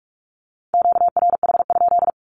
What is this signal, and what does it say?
Morse code, heard as a keyed tone, about 44 wpm, 700 Hz. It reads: QRH?
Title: Q-Codes 45wpm